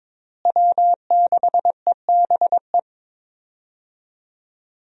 Morse code, heard as a keyed tone, about 22 wpm, 700 Hz. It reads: W6EBE
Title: USA Call Signs - 5 Characters - Repeated 2x - 22wpm